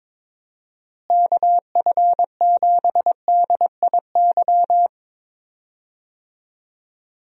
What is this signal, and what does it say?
Morse code, heard as a keyed tone, about 22 wpm, 700 Hz. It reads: KF7DIY
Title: USA Call Signs - 6 Characters - Repeated 2x - 22wpm